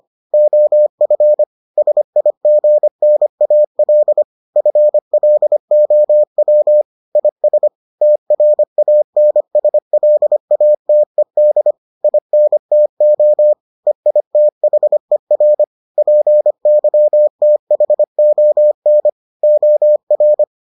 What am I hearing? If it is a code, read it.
Morse code, heard as a keyed tone, about 25 wpm, 600 Hz. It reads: OF SIGNAL FLOW IS TRANSLATED INTO EITHER PYTHON OR